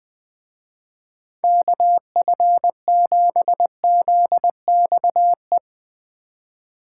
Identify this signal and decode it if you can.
Morse code, heard as a keyed tone, about 20 wpm, 700 Hz. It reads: KF7ZXE
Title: Speed Racing - USA 2x3 Call Signs 17wpm